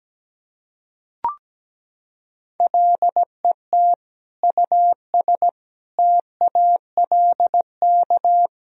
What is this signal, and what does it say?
Morse code, heard as a keyed tone, about 17 wpm, 700 Hz. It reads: LET US TALK
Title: Sentences from Top 400 Words and Abbreviations 17wpm